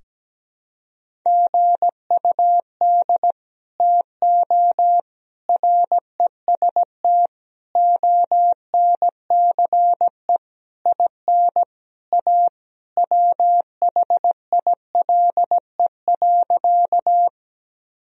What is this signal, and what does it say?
Morse code, heard as a keyed tone, about 17 wpm, 700 Hz. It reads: GUD TO REST ONCE IN A WHILE.
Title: Sentences from Top 500 Words and Abbreviations 17wpm